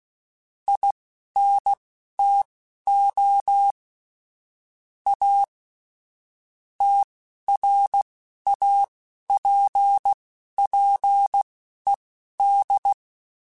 Morse code, heard as a keyed tone, about 16 wpm, spaced as for 11 wpm, 800 Hz. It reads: INTO A TRAPPED